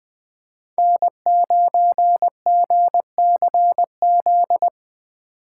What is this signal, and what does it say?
Morse code, heard as a keyed tone, about 20 wpm, 700 Hz. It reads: N9GCZ